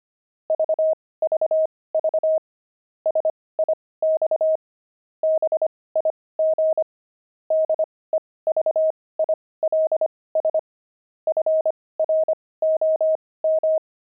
Morse code, heard as a keyed tone, about 25 wpm, spaced as for 18 wpm, 630 Hz. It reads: VVV SIX BIG DEVILS FROM
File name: pangram-fs-18-cs-25-devils